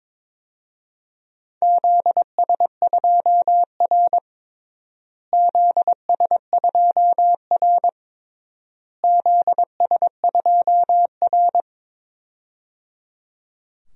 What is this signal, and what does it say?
Morse code, heard as a keyed tone, about 22 wpm, 700 Hz. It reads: ZS2R ZS2R ZS2R
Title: DXCC-100 Call Signs - 4 Characters - Repeated 3x - 22wpm